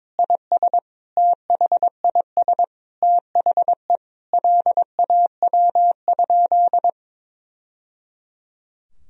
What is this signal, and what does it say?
Morse code, heard as a keyed tone, about 22 wpm, 700 Hz. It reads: IS THIS THE LAW?
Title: Sentences from Top 1000 Words - Review Part 3 - 22wpm